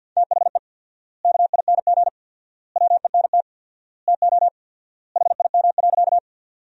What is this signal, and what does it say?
Morse code, heard as a keyed tone, about 46 wpm, 700 Hz. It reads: THE KING WENT TO HIM.
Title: 4x Word-Spacing - Sentences from Top 400 Words - 50wpm